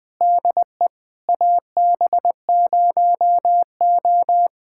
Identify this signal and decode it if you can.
Morse code, heard as a keyed tone, about 19 wpm, 690 Hz. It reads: DE AB0O
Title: POTA QSO Practice - 20wpm